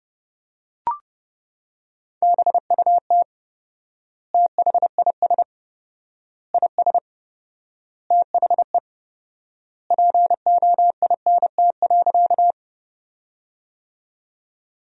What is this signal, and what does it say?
Morse code, heard as a keyed tone, about 30 wpm, 700 Hz. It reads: BUT THIS IS THE POINT.
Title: 4x Word-Spacing - Sentences from Top 200 Words - 30wpm